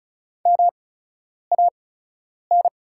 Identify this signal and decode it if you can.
Morse code, heard as a keyed tone, about 36 wpm, spaced as for 10 wpm, 690 Hz. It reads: MAN
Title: 8x Character Spacing - Top 200 Words - 35wpm